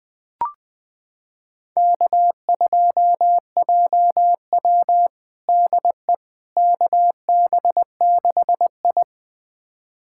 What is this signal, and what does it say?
Morse code, heard as a keyed tone, about 20 wpm, 700 Hz. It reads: K2JW DE KB6I
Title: USA Callsign DE Callsign - 4 Characters - 20wpm